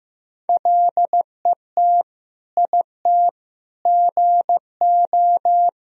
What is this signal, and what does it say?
Morse code, heard as a keyed tone, about 16 wpm, 700 Hz. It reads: LET IT GO
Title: Sentences from Top 1000 Words - Review Part 1 - 15wpm